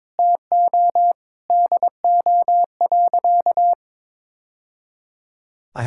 Morse code, heard as a keyed tone, about 22 wpm, 700 Hz. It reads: TO DO.
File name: practice / 22wpm / Sentences with 2-Word Phrases - 22wpm